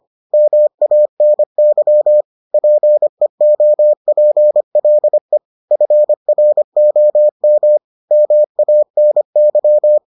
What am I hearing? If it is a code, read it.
Morse code, heard as a keyed tone, about 25 wpm, 600 Hz. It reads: MANY PEOPLE FROM MANY